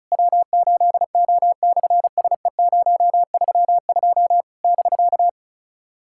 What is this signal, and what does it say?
Morse code, heard as a keyed tone, about 33 wpm, 700 Hz. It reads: W8O/SE032 <BK>
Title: SOTA QSO Practice 35wpm